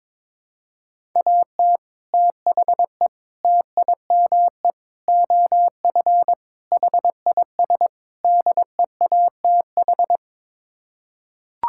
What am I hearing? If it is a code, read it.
Morse code, heard as a keyed tone, about 22 wpm, 700 Hz. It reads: AT THE TIME OF HIS DEATH
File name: practice / 22wpm / Sets of 6 Words - 22wpm